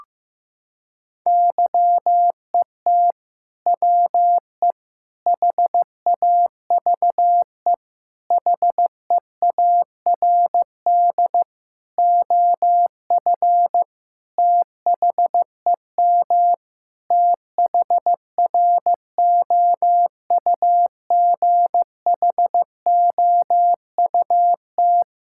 Morse code, heard as a keyed tone, about 15 wpm, 700 Hz. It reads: YET WE HAVE HEARD OF THEM THROUGHOUT
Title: Sentences from Top 1000 Words - Review Part 5 - 15wpm